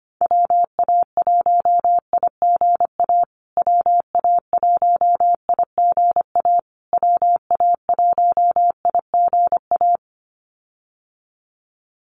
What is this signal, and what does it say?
Morse code, heard as a keyed tone, about 25 wpm, 700 Hz. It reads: WA1IGA WA1IGA WA1IGA
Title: USA Call Signs Repeated 3x 25wpm